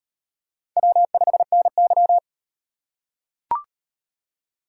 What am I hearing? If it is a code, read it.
Morse code, heard as a keyed tone, about 38 wpm, 700 Hz. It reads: W5NY